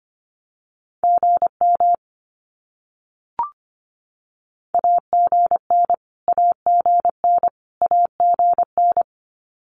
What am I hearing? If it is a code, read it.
Morse code, heard as a keyed tone, about 25 wpm, 700 Hz. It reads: GM AGN AGN AGN
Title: Abbreviations Repeated 3x - 25wpm